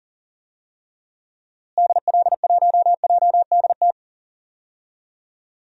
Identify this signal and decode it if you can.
Morse code, heard as a keyed tone, about 40 wpm, 700 Hz. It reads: DL1JDT
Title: DXCC-100 Call Signs - 6 Characters - Repeated 3x - 40wpm